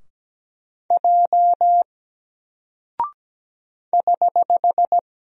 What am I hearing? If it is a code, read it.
Morse code, heard as a keyed tone, about 17 wpm, 700 Hz. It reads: J <HH>
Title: Single Character (Letter:Number:Punctuation:Prosign) Alt 17wpm